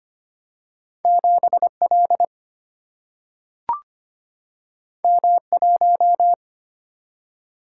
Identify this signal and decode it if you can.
Morse code, heard as a keyed tone, about 25 wpm, 700 Hz. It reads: 7 L M 1
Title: Single-Letter Single-Number 25wpm